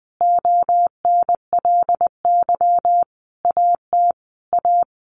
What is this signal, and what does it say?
Morse code, heard as a keyed tone, about 20 wpm, 700 Hz. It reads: ONLY AT A
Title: Sentences from Top 1400 Words 20wpm